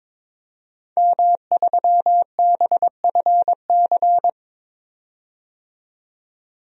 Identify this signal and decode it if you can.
Morse code, heard as a keyed tone, about 22 wpm, 700 Hz. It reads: M3BFC
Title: DXCC-100 Call Signs - 5 Characters - 22wpm